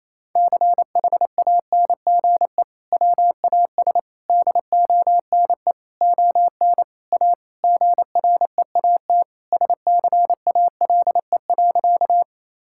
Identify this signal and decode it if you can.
Morse code, heard as a keyed tone, about 28 wpm, 700 Hz. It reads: CHANGE WAS DONE ON A GREAT SCALE.
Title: Sentences from Top 1000 Words - Review Part 5 - 28wpm